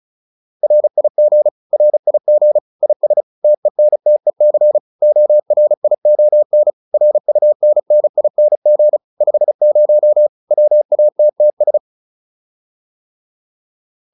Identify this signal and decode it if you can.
Morse code, heard as a keyed tone, about 34 wpm, 600 Hz. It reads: RIG RIG IS TENTEC ORION RUNNING 50 WATTS
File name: QSO 402 35